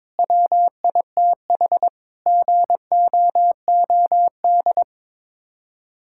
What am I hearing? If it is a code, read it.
Morse code, heard as a keyed tone, about 22 wpm, 700 Hz. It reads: WITH GOOD
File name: practice / 22wpm / Sets of 8 Words - 22wpm